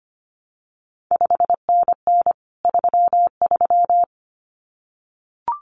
Morse code, heard as a keyed tone, about 25 wpm, 700 Hz. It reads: 5NN 33
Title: CQ WW DX Contest Exchange Encore - 25wpm